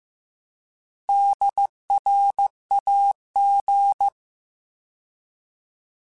Morse code, heard as a keyed tone, about 15 wpm, 800 Hz. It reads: DRAG